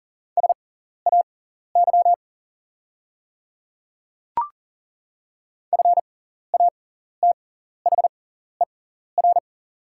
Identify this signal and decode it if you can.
Morse code, heard as a keyed tone, about 38 wpm, spaced as for 14 wpm, 690 Hz. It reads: SAY FATHER
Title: 6x Character Spacing - Top 200 Words - 40wpm